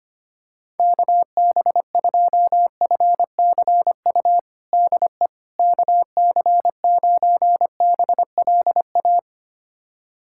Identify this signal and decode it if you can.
Morse code, heard as a keyed tone, about 25 wpm, 700 Hz. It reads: KB2FCU DE KC9BLA
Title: USA Callsign DE Callsign 25wpm